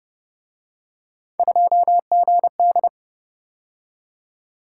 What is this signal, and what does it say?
Morse code, heard as a keyed tone, about 29 wpm, 700 Hz. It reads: 2GD